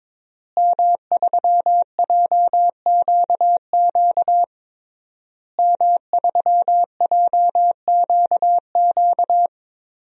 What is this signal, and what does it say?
Morse code, heard as a keyed tone, about 22 wpm, 700 Hz. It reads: M3JQQ M3JQQ
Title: DXCC-100 Call Signs - 5 Characters - Repeated 3x - 22wpm